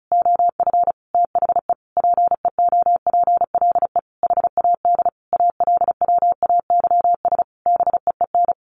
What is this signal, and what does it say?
Morse code, heard as a keyed tone, about 34 wpm, 700 Hz. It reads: OF THE PEOPLE HAD ALWAYS BEEN
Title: Sentences from Top 1400 Words 35wpm